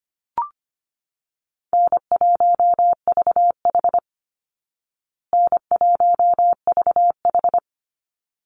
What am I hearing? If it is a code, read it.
Morse code, heard as a keyed tone, about 25 wpm, 700 Hz. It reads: N1VH N1VH
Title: Easy Call Signs Repeated 2x - 25wpm